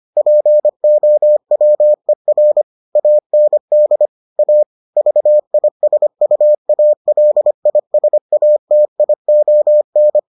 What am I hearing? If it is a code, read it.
Morse code, heard as a keyed tone, about 25 wpm, 600 Hz. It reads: POWER AND A VISUALISATION